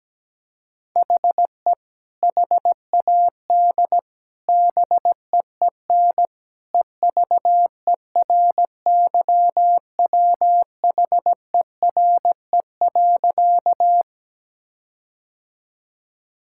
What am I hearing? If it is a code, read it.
Morse code, heard as a keyed tone, about 17 wpm, 700 Hz. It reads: HE HAD BEEN EVERYWHERE.
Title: Sentences from Top 1600 Words 17wpm